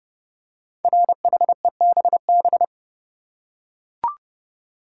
Morse code, heard as a keyed tone, about 30 wpm, 700 Hz. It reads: RHEBB
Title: Code Groups - 5 Letters - 30wpm